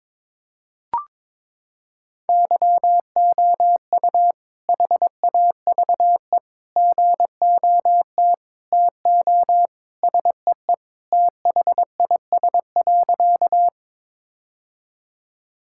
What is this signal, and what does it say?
Morse code, heard as a keyed tone, about 21 wpm, 700 Hz. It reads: YOU HAVE GOT TO SEE THIS.
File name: practice / 22wpm / Sentences with 2-Word Phrases - 22wpm